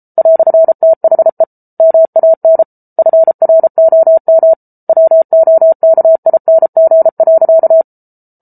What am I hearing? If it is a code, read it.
Morse code, heard as a keyed tone, about 34 wpm, 650 Hz. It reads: "THE MAN FROM WOKING.